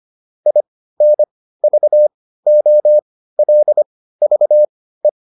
Morse code, heard as a keyed tone, about 25 wpm, spaced as for 15 wpm, 600 Hz. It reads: INVOLVE